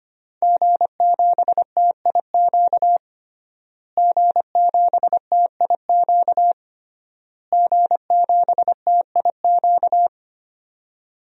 Morse code, heard as a keyed tone, about 25 wpm, 700 Hz. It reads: G7TIQ G7TIQ G7TIQ
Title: DXCC-100 Call Signs - 5 Characters - Repeated 3x - 25wpm